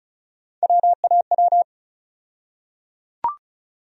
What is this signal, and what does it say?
Morse code, heard as a keyed tone, about 34 wpm, 700 Hz. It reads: WAW
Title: USA Call Sign Postfixes - 3 Letters 35wpm